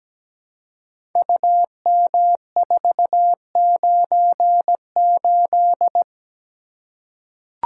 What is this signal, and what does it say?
Morse code, heard as a keyed tone, about 17 wpm, 700 Hz. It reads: UM498
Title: Code Groups - 5 Letter:Numbers - 17wpm